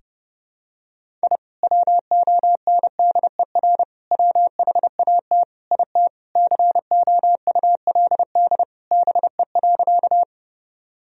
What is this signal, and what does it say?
Morse code, heard as a keyed tone, about 30 wpm, 700 Hz. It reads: I WONDER WHAT IT COULD BE.
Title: Sentences from Top 1000 Words - Review Part 3 - 30wpm